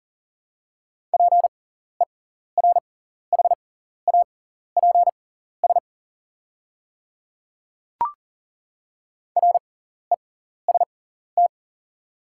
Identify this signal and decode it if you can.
Morse code, heard as a keyed tone, about 40 wpm, spaced as for 14 wpm, 690 Hz. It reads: PERHAPS REST
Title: 6x Character Spacing - Top 500 Words - 40wpm